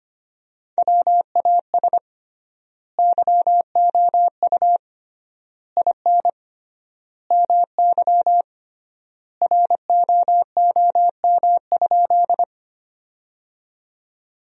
Morse code, heard as a keyed tone, about 25 wpm, 700 Hz. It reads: WAS YOU IN MY ROOM?
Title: 3x Word-Spacing - Sentences from Top 300 Words - 25wpm